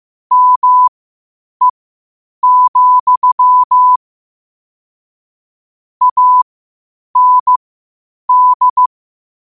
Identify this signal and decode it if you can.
Morse code, heard as a keyed tone, about 15 wpm, spaced as for 8 wpm, 1000 Hz. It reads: ME, AND
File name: qotdfiles / 8wpm